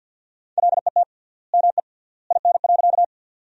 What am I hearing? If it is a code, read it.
Morse code, heard as a keyed tone, about 50 wpm, 700 Hz. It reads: LET ME IN.
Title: 3x Word-Spacing - Sentences from Top 300 Words - 50wpm